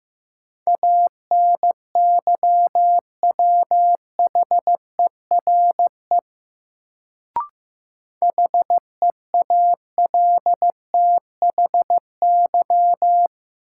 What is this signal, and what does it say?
Morse code, heard as a keyed tone, about 15 wpm, 700 Hz. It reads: ANYWHERE HEALTHY
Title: Sentences from Top 1400 Words - New Word Drill Alt - 15wpm